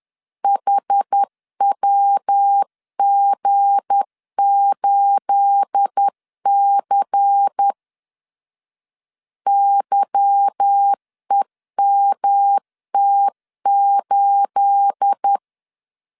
Morse code, heard as a keyed tone, about 10 wpm, 800 Hz. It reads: HWG8C YEMT8
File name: LEZIONI / 9 28 10wpm 2842